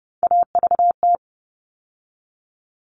Morse code, heard as a keyed tone, about 30 wpm, 700 Hz. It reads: AVT
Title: All Cut Numbers - 3-Digit 30wpm